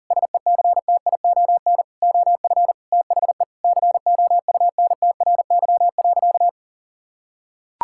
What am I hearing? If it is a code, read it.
Morse code, heard as a keyed tone, about 40 wpm, 700 Hz. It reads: SECTION OF THE COUNTRY.
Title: Sentences from Top 1000 Words - Review Part 5 - 40wpm